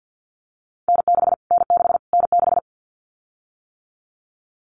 Morse code, heard as a keyed tone, about 50 wpm, 700 Hz. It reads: N6 N6 N6
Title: Callsign Prefixes for Top 25 DXCC - Repeated 3x - 50wpm